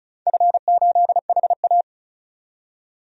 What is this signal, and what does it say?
Morse code, heard as a keyed tone, about 36 wpm, 690 Hz. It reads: F8HA
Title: DXCC-100 Call Signs - 4 Characters - 35wpm